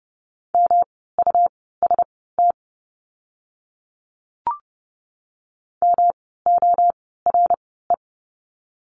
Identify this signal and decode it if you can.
Morse code, heard as a keyed tone, about 30 wpm, spaced as for 17 wpm, 700 Hz. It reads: MUST MORE
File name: course / 30:10wpm / 237 Lesson 38f - Letter q - Words 30wpm